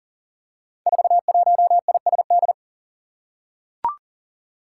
Morse code, heard as a keyed tone, about 37 wpm, 700 Hz. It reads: 41ISD
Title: Five Challenging Letter-Number Combinations 40wpm